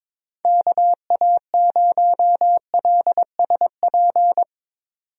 Morse code, heard as a keyed tone, about 22 wpm, 700 Hz. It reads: KA0LSP
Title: USA Call Signs - 6 Characters - 22wpm